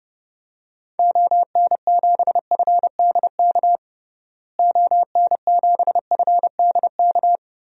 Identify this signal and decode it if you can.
Morse code, heard as a keyed tone, about 32 wpm, 700 Hz. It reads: ON7FDK ON7FDK
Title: DXCC-100 Call Signs - 6 Characters - Repeated 3x - 30wpm